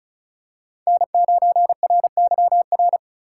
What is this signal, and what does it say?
Morse code, heard as a keyed tone, about 36 wpm, 700 Hz. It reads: N9RYR
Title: USA Call Signs - 4x Character Spacing - 35wpm